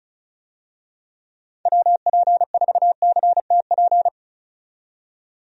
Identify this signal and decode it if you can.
Morse code, heard as a keyed tone, about 34 wpm, 700 Hz. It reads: WP4CTP
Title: USA Call Signs - 6 Characters - 35wpm